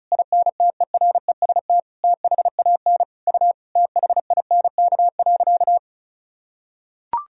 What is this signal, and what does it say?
Morse code, heard as a keyed tone, about 34 wpm, 700 Hz. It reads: INTEREST THAN U THINK.
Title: Sentences from Top 400 Words and Abbreviations 35wpm